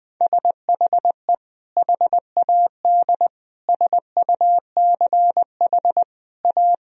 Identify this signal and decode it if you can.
Morse code, heard as a keyed tone, about 21 wpm, 700 Hz. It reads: SHE HAD SUCH A
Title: Sentences from Top 1000 Words - Review Part 1 - 20wpm